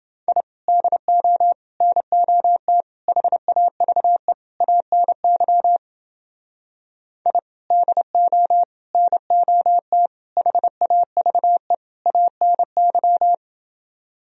Morse code, heard as a keyed tone, about 28 wpm, 700 Hz. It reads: I DO NOT HAVE ANY I DO NOT HAVE ANY